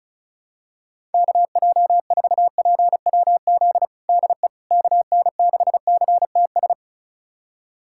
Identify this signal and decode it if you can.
Morse code, heard as a keyed tone, about 36 wpm, 690 Hz. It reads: KJ4PWZ DE KN6CTS
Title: USA Callsign DE Callsign 35wpm